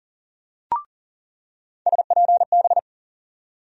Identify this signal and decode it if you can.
Morse code, heard as a keyed tone, about 40 wpm, 700 Hz. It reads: SPB
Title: USA Call Sign Postfixes - 3 Letters 40wpm